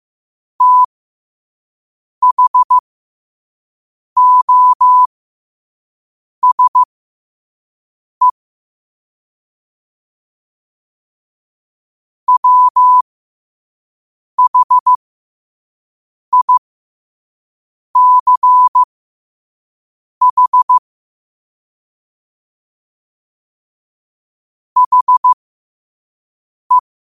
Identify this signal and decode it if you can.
Morse code, heard as a keyed tone, about 15 wpm, spaced as for 5 wpm, 1000 Hz. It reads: THOSE WHICH HE
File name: qotdfiles / 5wpm